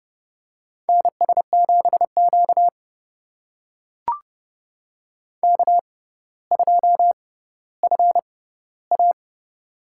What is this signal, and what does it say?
Morse code, heard as a keyed tone, about 32 wpm, spaced as for 16 wpm, 700 Hz. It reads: NS7Q K2FA
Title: USA Call Signs - 4 Characters - 6x Character Spacing - 30wpm